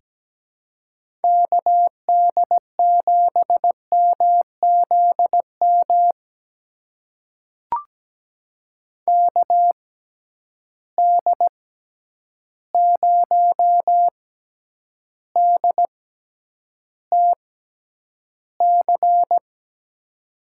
Morse code, heard as a keyed tone, about 17 wpm, spaced as for 9 wpm, 700 Hz. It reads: KD7MZM KD0DTC
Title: USA Call Signs - 6 Characters - 6x Character Spacing - 17wpm